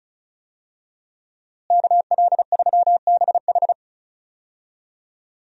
Morse code, heard as a keyed tone, about 36 wpm, 700 Hz. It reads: KL3BH
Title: USA Call Signs - 5 Characters - Repeated 3x - 35wpm